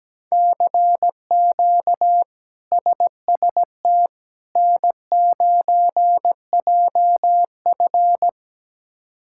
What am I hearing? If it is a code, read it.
Morse code, heard as a keyed tone, about 17 wpm, 700 Hz. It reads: CQ SST N9JF
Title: K1USN SST QSO Practice - 17wpm